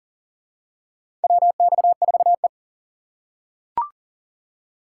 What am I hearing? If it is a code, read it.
Morse code, heard as a keyed tone, about 40 wpm, 700 Hz. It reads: WX4E
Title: USA Call Signs - 4 Characters - 6x Character Spacing - 40wpm